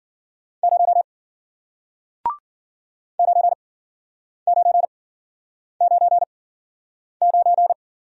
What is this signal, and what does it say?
Morse code, heard as a keyed tone, about 50 wpm, spaced as for 9 wpm, 700 Hz. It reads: 0 9999